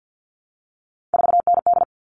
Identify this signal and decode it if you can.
Morse code, heard as a keyed tone, about 50 wpm, 700 Hz. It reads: 4ND
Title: All Cut Numbers - 3-Digit 50wpm